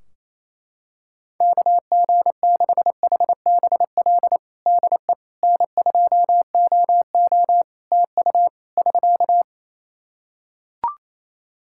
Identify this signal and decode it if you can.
Morse code, heard as a keyed tone, about 28 wpm, 700 Hz. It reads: KG6HBL DE N2OO TU <SK>
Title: K1USN SST QSO Practice 28wpm